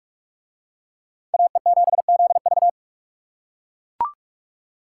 Morse code, heard as a keyed tone, about 39 wpm, 700 Hz. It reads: AE7ZV